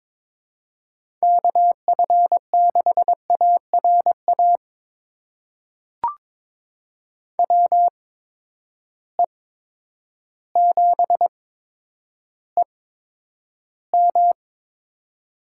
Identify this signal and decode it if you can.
Morse code, heard as a keyed tone, about 22 wpm, spaced as for 10 wpm, 700 Hz. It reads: KF6ARA WE7EM
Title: USA Call Signs - 8x Character Spacing - 22wpm